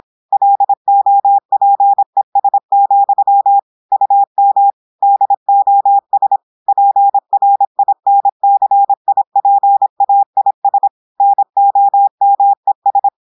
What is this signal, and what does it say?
Morse code, heard as a keyed tone, about 26 wpm, 800 Hz. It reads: LOPES, UM DOS PRINCIPAIS NOMES
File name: audios / portugais / portugais texte en clair 1 vitesse 26 mots minutes